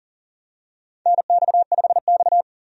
Morse code, heard as a keyed tone, about 40 wpm, 700 Hz. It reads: NX5X
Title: USA Call Signs - 4 Characters - 40wpm